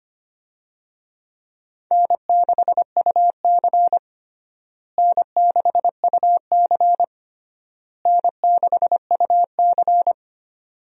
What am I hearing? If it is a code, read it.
Morse code, heard as a keyed tone, about 25 wpm, 700 Hz. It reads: N6UC N6UC N6UC
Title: USA Call Signs - 4 Characters - Repeated 3x - 25wpm